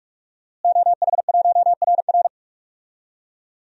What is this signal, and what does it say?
Morse code, heard as a keyed tone, about 41 wpm, 700 Hz. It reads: OH1RR
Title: DXCC-100 Call Signs - 5 Characters - Repeated 3x - 45wpm